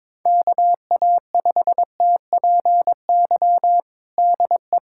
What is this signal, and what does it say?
Morse code, heard as a keyed tone, about 22 wpm, 700 Hz. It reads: KA5TPY DE
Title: USA Callsign DE Callsign - 6 Characters - 22wpm